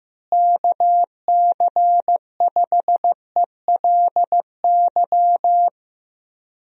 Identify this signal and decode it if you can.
Morse code, heard as a keyed tone, about 15 wpm, 700 Hz. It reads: KC5ELY
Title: USA Call Signs - 6 Characters - Repeated 2x - 15wpm